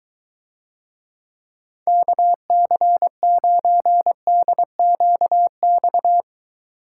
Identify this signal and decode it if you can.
Morse code, heard as a keyed tone, about 23 wpm, 700 Hz. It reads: KC9DQX